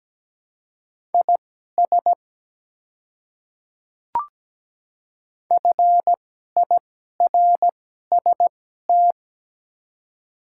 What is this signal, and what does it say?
Morse code, heard as a keyed tone, about 17 wpm, spaced as for 12 wpm, 700 Hz. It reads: IS FIRST
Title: 2x Character Spacing - Top 100 Words - 17wpm